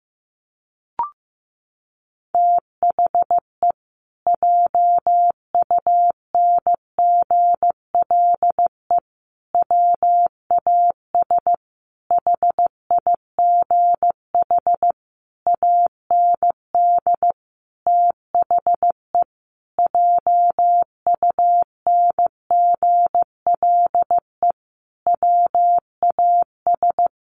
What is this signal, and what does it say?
Morse code, heard as a keyed tone, about 15 wpm, 700 Hz. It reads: THE JUNGLE WAS HIGH AND THE JUNGLE WAS
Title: A Sound Of Thunder 15wpm